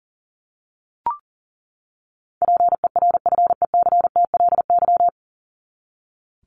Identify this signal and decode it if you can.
Morse code, heard as a keyed tone, about 40 wpm, 700 Hz. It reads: PERFECTLY
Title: LY Suffix - Top 100 Words with Base Words 40wpm